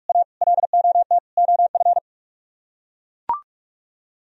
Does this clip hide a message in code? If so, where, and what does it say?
Yes: Morse code from 0.1 to 2.0 s, heard as a keyed tone, about 43 wpm, 700 Hz. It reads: A LOT OF